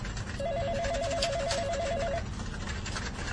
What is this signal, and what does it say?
There is no code.